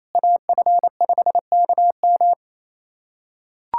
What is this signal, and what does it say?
Morse code, heard as a keyed tone, about 28 wpm, 700 Hz. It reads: AF5KM
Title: USA Callsign DE Callsign 28wpm